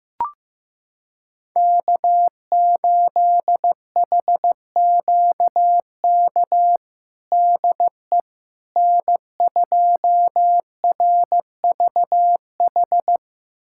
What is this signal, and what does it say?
Morse code, heard as a keyed tone, about 15 wpm, 700 Hz. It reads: K8HQK DE N2RVH
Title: USA Callsign DE Callsign - 5 Characters - 15wpm